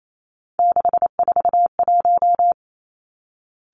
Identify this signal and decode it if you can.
Morse code, heard as a keyed tone, about 28 wpm, 700 Hz. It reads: 641
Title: Three Challenging Letter-Number Combinations 28wpm